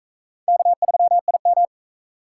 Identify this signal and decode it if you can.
Morse code, heard as a keyed tone, about 39 wpm, 700 Hz. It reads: K3IM